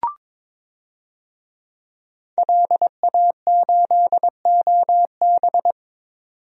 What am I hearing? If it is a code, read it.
Morse code, heard as a keyed tone, about 22 wpm, 700 Hz. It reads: LA8OB
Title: DXCC-100 Call Signs - 5 Characters - 22wpm